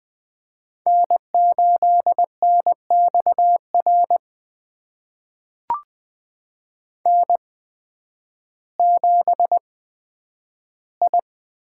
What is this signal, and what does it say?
Morse code, heard as a keyed tone, about 20 wpm, spaced as for 11 wpm, 700 Hz. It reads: N8NXR N7I
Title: USA Call Signs - 5 Characters - 8x Character Spacing - 20wpm